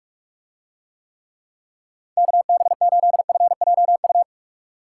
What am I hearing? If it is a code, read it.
Morse code, heard as a keyed tone, about 45 wpm, 700 Hz. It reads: KB8FJU